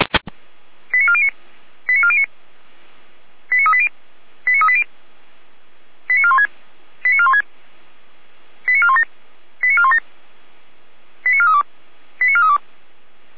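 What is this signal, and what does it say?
Five-tone selective call (ZVEI), heard as quick runs of five tones, tones of about 70 ms. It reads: 89409 89409 89380 89380 89416 89416 89417 89417 89432 89432
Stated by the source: ZVEI five-tone selective call 89560 received by SDR.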